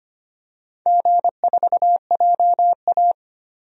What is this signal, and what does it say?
Morse code, heard as a keyed tone, about 25 wpm, 700 Hz. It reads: G4JA
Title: DXCC-100 Call Signs - 4 Characters - Repeated 3x - 25wpm